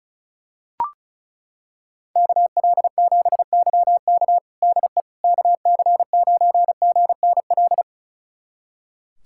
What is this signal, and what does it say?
Morse code, heard as a keyed tone, about 36 wpm, 700 Hz. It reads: KL7YK DE KC9GNL
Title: USA Callsign DE Callsign 35wpm